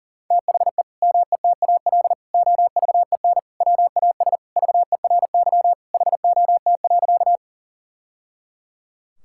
Morse code, heard as a keyed tone, about 40 wpm, 700 Hz. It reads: THE METAL OVEN WAS VERY HOT.
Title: Sentences from Top 1000 Words - Review Part 6 - 40wpm